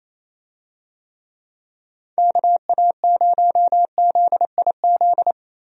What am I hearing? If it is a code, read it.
Morse code, heard as a keyed tone, about 28 wpm, 700 Hz. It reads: KA0ZIZ
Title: USA Call Signs - 6 Characters - Repeated 3x - 28wpm